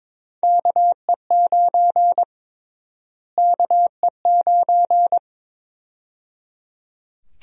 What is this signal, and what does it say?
Morse code, heard as a keyed tone, about 22 wpm, 700 Hz. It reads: KE9 KE9
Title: USA Call Sign Prefixes - Repeated 2x - 22wpm